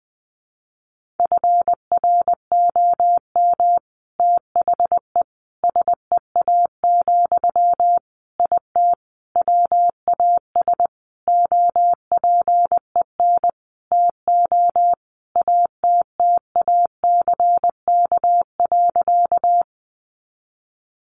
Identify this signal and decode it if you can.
Morse code, heard as a keyed tone, about 20 wpm, 700 Hz. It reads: FROM THE SEA, IT WAS OPEN TO ATTACK.
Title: Sentences from Top 1500 Words 20wpm